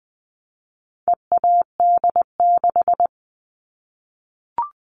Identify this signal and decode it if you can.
Morse code, heard as a keyed tone, about 20 wpm, 700 Hz. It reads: EAD6
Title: All Cut Numbers - 4-Digit 20wpm